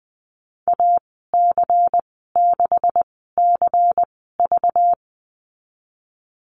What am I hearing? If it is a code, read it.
Morse code, heard as a keyed tone, about 19 wpm, spaced as for 14 wpm, 700 Hz. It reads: AC6CV
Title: 193 Lesson 31g - Letter v - Call Signs 20wpm